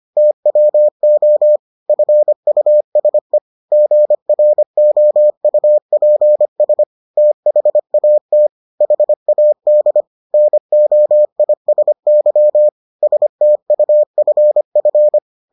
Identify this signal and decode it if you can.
Morse code, heard as a keyed tone, about 25 wpm, 600 Hz. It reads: TWO FUSE GROUPS THAT HAD NOISY STUFF